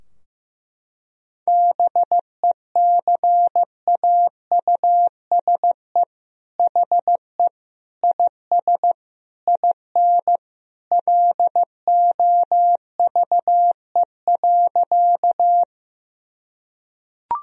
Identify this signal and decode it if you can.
Morse code, heard as a keyed tone, about 16 wpm, 700 Hz. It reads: BECAUSE HE IS IN LOVE.
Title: Sentences from Top 1000 Words - Review Part 3 - 15wpm